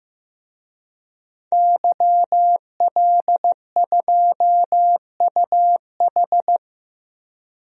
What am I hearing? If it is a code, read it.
Morse code, heard as a keyed tone, about 16 wpm, 700 Hz. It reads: YL2UH
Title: DXCC-100 Call Signs - 5 Characters - 15wpm